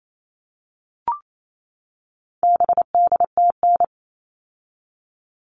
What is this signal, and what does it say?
Morse code, heard as a keyed tone, about 28 wpm, 700 Hz. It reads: BDTN
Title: All Cut Numbers - 4-Digit 28wpm